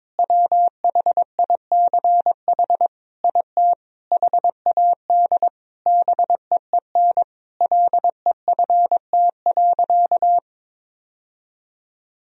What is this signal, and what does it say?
Morse code, heard as a keyed tone, about 22 wpm, 700 Hz. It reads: WHICH IT HAD BEEN LEFT.
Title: Sentences from Top 1000 Words - Review Part 2 - 22wpm